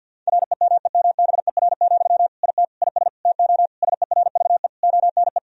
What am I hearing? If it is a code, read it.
Morse code, heard as a keyed tone, about 50 wpm, 700 Hz. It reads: REMEMBER, IT IS TO SERVE ONE
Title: Sentences from Top 1000 Words - Review Part 2 - 50wpm